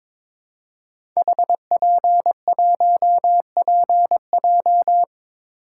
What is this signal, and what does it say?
Morse code, heard as a keyed tone, about 22 wpm, 700 Hz. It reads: HP1PJ
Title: Code Groups - 5 Letter:Numbers - 22wpm